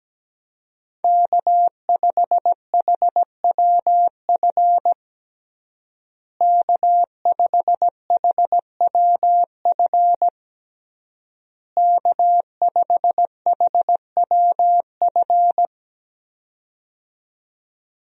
Morse code, heard as a keyed tone, about 17 wpm, 700 Hz. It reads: K5HWF K5HWF K5HWF
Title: USA Call Signs - 5 Characters - Repeated 3x - 17wpm